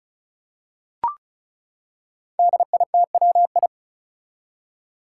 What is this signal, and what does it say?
Morse code, heard as a keyed tone, about 36 wpm, 700 Hz. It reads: DITWI
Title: Five Challenging Letter-Number Combinations 35wpm